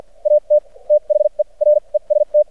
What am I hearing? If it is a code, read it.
Morse code, heard as a keyed tone, about 56 wpm, spaced as for 40 wpm, 600 Hz. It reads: AT THE REST
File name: archived / CRSnet